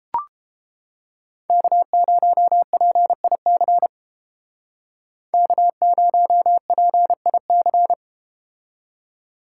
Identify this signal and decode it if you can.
Morse code, heard as a keyed tone, about 31 wpm, 700 Hz. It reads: K0PIC K0PIC